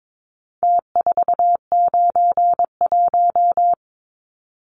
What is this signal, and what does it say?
Morse code, heard as a keyed tone, about 22 wpm, 700 Hz. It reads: T491
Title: Numbers - Contest Style - 4-Digit 22wpm